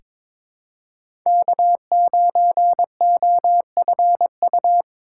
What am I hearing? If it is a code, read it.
Morse code, heard as a keyed tone, about 22 wpm, 700 Hz. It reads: K9OFU
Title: USA Callsign DE Callsign - 5 Characters - 22wpm